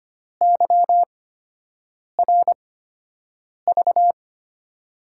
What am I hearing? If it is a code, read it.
Morse code, heard as a keyed tone, about 25 wpm, spaced as for 7 wpm, 700 Hz. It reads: YRV
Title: USA Call Signs - 6 Characters - 8x Character Spacing - 25wpm